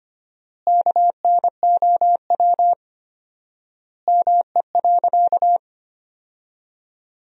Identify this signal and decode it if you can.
Morse code, heard as a keyed tone, about 25 wpm, 700 Hz. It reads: KNOW ME.
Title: 4x Word-Spacing - Sentences from Top 200 Words - 25wpm